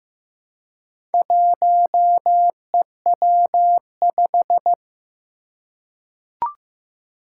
Morse code, heard as a keyed tone, about 15 wpm, 700 Hz. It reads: 1EW5
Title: Four Challenging Letter-Number Combinations 15wpm